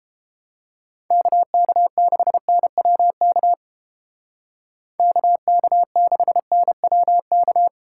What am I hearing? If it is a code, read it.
Morse code, heard as a keyed tone, about 29 wpm, 700 Hz. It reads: KK6NWK KK6NWK